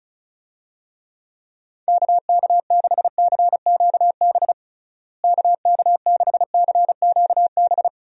Morse code, heard as a keyed tone, about 34 wpm, 700 Hz. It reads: KK6CQB KK6CQB
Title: USA Call Signs - 6 Characters - Repeated 2x - 35wpm